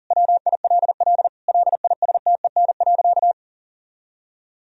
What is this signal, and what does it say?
Morse code, heard as a keyed tone, about 40 wpm, 700 Hz. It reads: WILL LISTEN.
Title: Sentences from Top 1000 Words - Review Part 3 - 40wpm